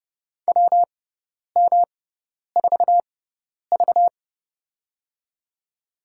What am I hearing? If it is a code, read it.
Morse code, heard as a keyed tone, about 30 wpm, spaced as for 10 wpm, 700 Hz. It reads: WM4V
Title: USA Call Signs - 4 Characters - 6x Character Spacing - 30wpm